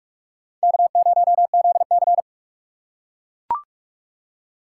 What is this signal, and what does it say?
Morse code, heard as a keyed tone, about 42 wpm, 700 Hz. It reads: K0ZC
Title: USA Call Signs - 4 Characters - 6x Character Spacing - 45wpm